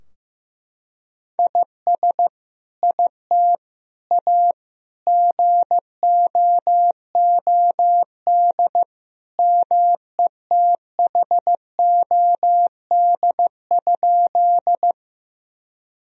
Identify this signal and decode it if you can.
Morse code, heard as a keyed tone, about 15 wpm, 700 Hz. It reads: IS IT A GOOD METHOD?
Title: Sentences from Top 1000 Words - Review Part 3 - 15wpm